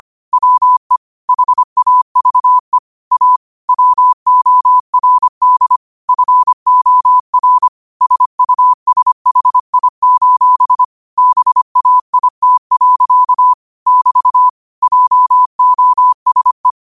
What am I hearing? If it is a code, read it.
Morse code, heard as a keyed tone, about 25 wpm, 1000 Hz. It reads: WE HAVE A WORD FOR SUSHI: BAIT. <BT> JOSE
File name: qotdfiles / 25wpm